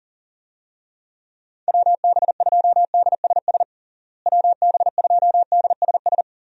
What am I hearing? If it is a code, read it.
Morse code, heard as a keyed tone, about 40 wpm, 700 Hz. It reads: WB2DSS WB2DSS
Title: USA Call Signs - 6 Characters - Repeated 2x - 40wpm